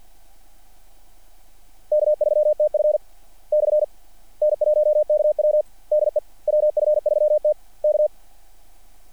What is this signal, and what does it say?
Morse code, heard as a keyed tone, about 50 wpm, 600 Hz. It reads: K3TF / N1KW DE WF3T K
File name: archived / CRSnet